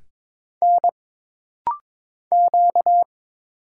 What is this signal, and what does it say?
Morse code, heard as a keyed tone, about 23 wpm, 700 Hz. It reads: N Q
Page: This rapid-fire timing is geared to push your ICR (Instant Character Recognition) skills.